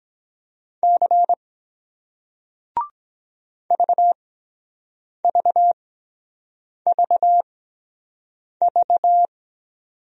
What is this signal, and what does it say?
Morse code, heard as a keyed tone, about 23 wpm, spaced as for 7 wpm, 700 Hz. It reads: C VVVV